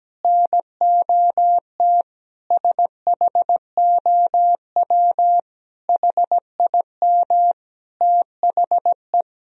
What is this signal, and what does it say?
Morse code, heard as a keyed tone, about 17 wpm, 700 Hz. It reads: NOT SHOW HIM THE
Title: Sentences from Top 1000 Words - Review Part 3 - 17wpm